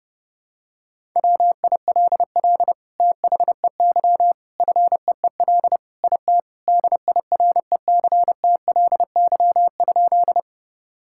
Morse code, heard as a keyed tone, about 30 wpm, 700 Hz. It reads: WILL THEY FEEL IT DIRECTLY?
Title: Sentences from Top 1000 Words - Review Part 5 - 30wpm